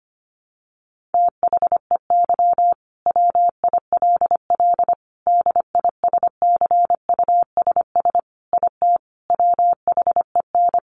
Morse code, heard as a keyed tone, about 25 wpm, 700 Hz. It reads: THEY WILL DISCUSS IT WHEN
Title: Sentences from Top 1400 Words 25wpm